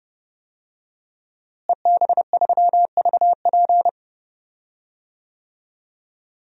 Morse code, heard as a keyed tone, about 30 wpm, 700 Hz. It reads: EB3VP
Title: DXCC-100 Call Signs - 5 Characters - Repeated 3x - 30wpm